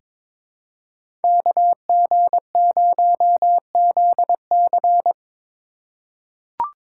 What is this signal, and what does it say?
Morse code, heard as a keyed tone, about 22 wpm, 700 Hz. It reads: KG0ZC
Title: USA Call Signs - 5 Characters - 4x Character Spacing - 22wpm